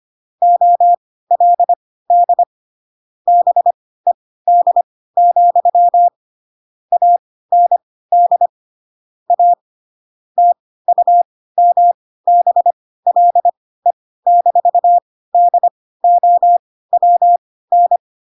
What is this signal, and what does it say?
Morse code, heard as a keyed tone, about 25 wpm, spaced as for 16 wpm, 700 Hz. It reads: OLD BED, AND A TUMBLE-DOWN
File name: Pinocchio / Pinocchio ch3 10 25 ik1hge